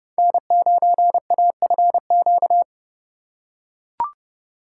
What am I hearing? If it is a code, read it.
Morse code, heard as a keyed tone, about 28 wpm, 700 Hz. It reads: N9AFQ